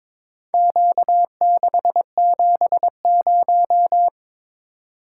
Morse code, heard as a keyed tone, about 22 wpm, 700 Hz. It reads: Q670
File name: practice / 22wpm / Four Letter-Number Combinations - 22wpm